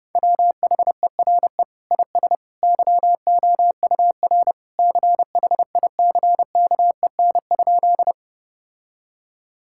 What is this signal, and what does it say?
Morse code, heard as a keyed tone, about 30 wpm, 700 Hz. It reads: WHERE IS YOUR CHICKEN?
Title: Sentences from Top 1000 Words - Review Part 5 - 30wpm